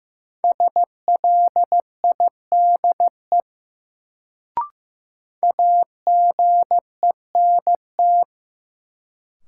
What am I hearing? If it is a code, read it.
Morse code, heard as a keyed tone, about 15 wpm, 700 Hz. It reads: SLIDE AGENT
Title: Sentences from Top 1600 Words - New Word Drill - 15wpm